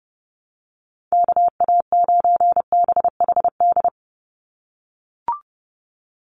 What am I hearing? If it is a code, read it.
Morse code, heard as a keyed tone, about 30 wpm, 700 Hz. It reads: KA9BHD
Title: Call Signs - 30wpm